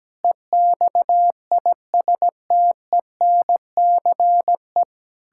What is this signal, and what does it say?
Morse code, heard as a keyed tone, about 17 wpm, 700 Hz. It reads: EXISTENCE
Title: Pangram Challenge v2 - 17wpm